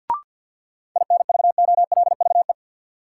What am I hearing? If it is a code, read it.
Morse code, heard as a keyed tone, about 53 wpm, 700 Hz. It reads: INVOLVE
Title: Sentences from Top 1200 Words - New Word Drill Alt - 50wpm